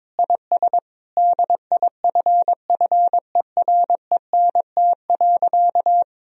Morse code, heard as a keyed tone, about 22 wpm, 700 Hz. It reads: IS DIFFERENT.
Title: Sentences from Top 1000 Words - Review Part 3 - 22wpm